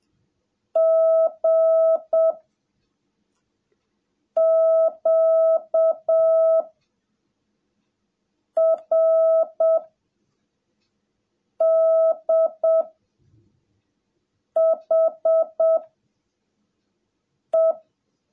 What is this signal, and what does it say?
Morse code, heard as a keyed tone, about 7 wpm, 660 Hz. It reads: G Q R D H E
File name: jh8cbh / 2021cw / cw17a